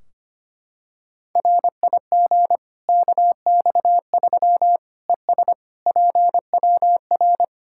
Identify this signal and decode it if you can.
Morse code, heard as a keyed tone, about 25 wpm, 700 Hz. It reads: RIG KX3 ES PWR
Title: Practice QSO - Exchange 2 - Info - 25wpm